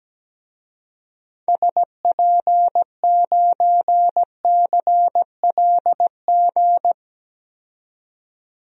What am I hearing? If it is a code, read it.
Morse code, heard as a keyed tone, about 17 wpm, 700 Hz. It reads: SP9CLG
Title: DXCC-100 Call Signs - 6 Characters - 17wpm